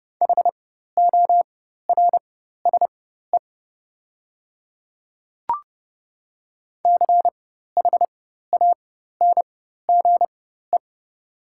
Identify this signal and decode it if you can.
Morse code, heard as a keyed tone, about 31 wpm, spaced as for 14 wpm, 690 Hz. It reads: HORSE CHANGE
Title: 4x Character Spacing - Top 500 Words - 30wpm